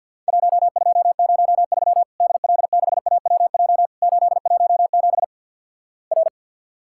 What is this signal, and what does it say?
Morse code, heard as a keyed tone, about 50 wpm, 700 Hz. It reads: 1203 DL6AWJ 817 R
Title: Worked All Europe DX Contest - QTC Exchange - 50wpm